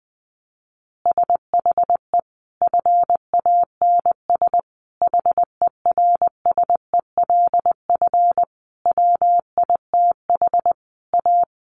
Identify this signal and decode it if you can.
Morse code, heard as a keyed tone, about 20 wpm, 700 Hz. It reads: SHE FANS HERSELF WITH A
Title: Additional Easy Phrases 20wpm